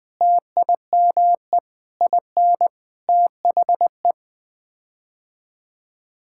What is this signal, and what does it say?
Morse code, heard as a keyed tone, about 20 wpm, 700 Hz. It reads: TIME IN THE
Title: Sets of 6 Words - 20wpm